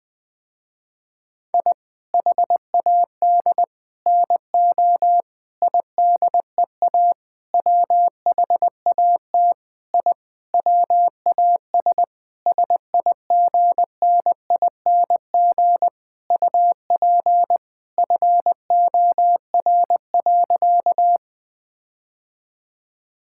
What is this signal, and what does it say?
Morse code, heard as a keyed tone, about 21 wpm, 700 Hz. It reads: I HAD NO IDEA WHAT I WAS SIGNING UP FOR.
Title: Sentences with 7-Word Phrases - Spoken Beginning - 20wpm